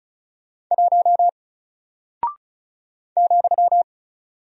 Morse code, heard as a keyed tone, about 36 wpm, 700 Hz. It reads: 1 ,
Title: Single Character (Letter:Number:Punctuation:Prosign) Alt 35wpm